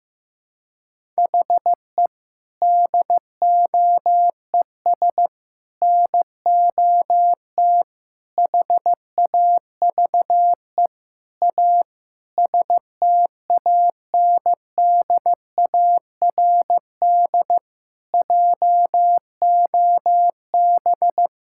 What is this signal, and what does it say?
Morse code, heard as a keyed tone, about 15 wpm, 700 Hz. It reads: HE DOES NOT HAVE A STANDARD JOB
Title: Sentences from Top 1000 Words - Review Part 6 - 15wpm